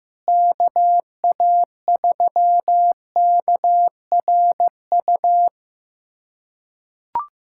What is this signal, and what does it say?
Morse code, heard as a keyed tone, about 16 wpm, 700 Hz. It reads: KA3KRU
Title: USA Callsign DE Callsign 15wpm